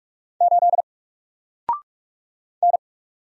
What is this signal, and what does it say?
Morse code, heard as a keyed tone, about 45 wpm, 700 Hz. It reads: 8 N
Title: Single Letter-Number 45wpm